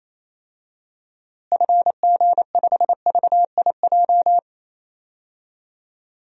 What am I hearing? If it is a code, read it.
Morse code, heard as a keyed tone, about 28 wpm, 700 Hz. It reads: FG5VIJ
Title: DXCC-100 Call Signs - 6 Characters - Repeated 2x - 28wpm